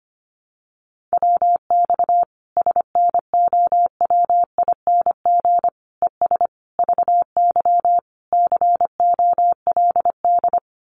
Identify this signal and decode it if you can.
Morse code, heard as a keyed tone, about 25 wpm, 700 Hz. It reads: WX SNOWING ES VY COLD
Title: Common QSO Phrases 25wpm